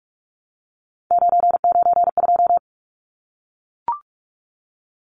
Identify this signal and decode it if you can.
Morse code, heard as a keyed tone, about 44 wpm, 700 Hz. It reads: 992
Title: Numbers - 3-Digit 45wpm